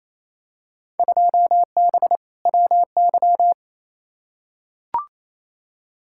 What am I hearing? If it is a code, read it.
Morse code, heard as a keyed tone, about 28 wpm, 700 Hz. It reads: 2B WY
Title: Field Day Exchanges Encore - 28wpm